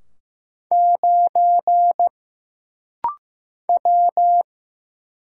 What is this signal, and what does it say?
Morse code, heard as a keyed tone, about 15 wpm, 700 Hz. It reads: 9 W
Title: Single Letter-Number Alt - Rapid-Fire 15wpm